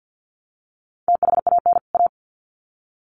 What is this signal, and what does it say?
Morse code, heard as a keyed tone, about 50 wpm, 700 Hz. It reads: THAN A
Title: Sets of 2 Words 50wpm